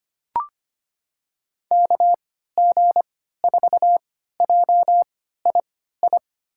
Morse code, heard as a keyed tone, about 25 wpm, spaced as for 14 wpm, 700 Hz. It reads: KG4JII
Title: USA Call Signs - 6 Characters - 3x Character Spacing - 25wpm